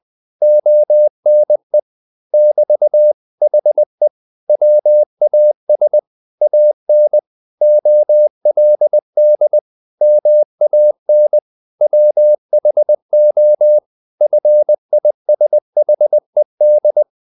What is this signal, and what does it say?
Morse code, heard as a keyed tone, about 20 wpm, 600 Hz. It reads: ONE <BT> HE WAS AN OLD MAN WHO FISHED
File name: the old man and the sea / the old man and the sea part 1 20wpm